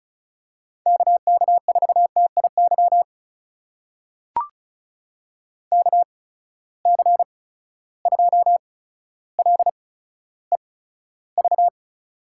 Morse code, heard as a keyed tone, about 34 wpm, spaced as for 15 wpm, 700 Hz. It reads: KK4TIY KC2LEV
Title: USA Call Signs - 6 Characters - 8x Character Spacing - 35wpm